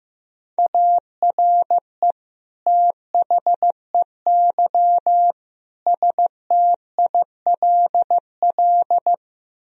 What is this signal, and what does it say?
Morse code, heard as a keyed tone, about 15 wpm, 700 Hz. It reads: ARE THEY STILL
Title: Sentences from Top 1000 Words - Review Part 1 - 15wpm